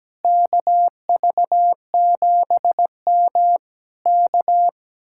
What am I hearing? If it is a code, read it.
Morse code, heard as a keyed tone, about 17 wpm, 700 Hz. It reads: KV7M K
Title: Speed-Racing Common QSO Phrases - 17wpm